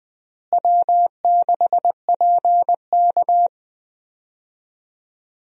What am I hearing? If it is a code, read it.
Morse code, heard as a keyed tone, about 20 wpm, 700 Hz. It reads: W6PK
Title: USA Call Signs - 4 Characters - Repeated 2x - 20wpm